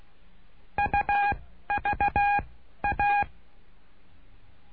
Morse code, heard as a keyed tone, about 16 wpm, 820 Hz. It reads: U V A
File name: cursos / CURSO DE TELEGRAFIA / Telegrafia - Trilha 09